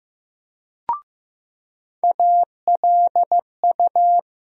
Morse code, heard as a keyed tone, about 15 wpm, 700 Hz. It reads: ALU
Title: USA Call Sign Postfixes - 3 Letters 15wpm